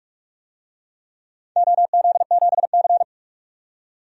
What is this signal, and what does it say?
Morse code, heard as a keyed tone, about 44 wpm, 700 Hz. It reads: OZ7C
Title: DXCC-100 Call Signs - 4 Characters - 45wpm